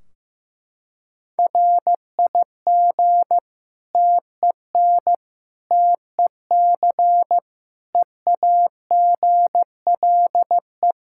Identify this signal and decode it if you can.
Morse code, heard as a keyed tone, about 16 wpm, 700 Hz. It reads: RIG TEN TEC EAGLE
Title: Practice QSO - Exchange 2 - Info - 15wpm